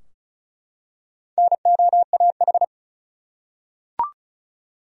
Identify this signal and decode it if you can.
Morse code, heard as a keyed tone, about 34 wpm, 700 Hz. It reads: NOAH
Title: USA Common Names - Max 4 Letters - 35wpm